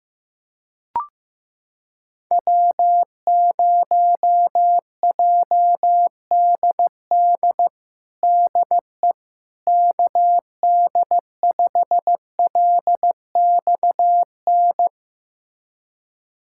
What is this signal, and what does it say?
Morse code, heard as a keyed tone, about 15 wpm, 700 Hz. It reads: W0JDD DE KD5LXN
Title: USA Callsign DE Callsign 15wpm